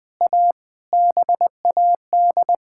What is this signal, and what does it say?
Morse code, heard as a keyed tone, about 20 wpm, 700 Hz. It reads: A BAD
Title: Sentences from Top 1600 Words 20wpm